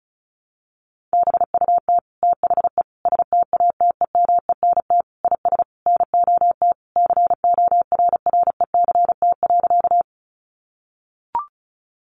Morse code, heard as a keyed tone, about 34 wpm, 700 Hz. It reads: BUT THE STATEMENT IS NOT CORRECT.
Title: Sentences from Top 1200 Words 35wpm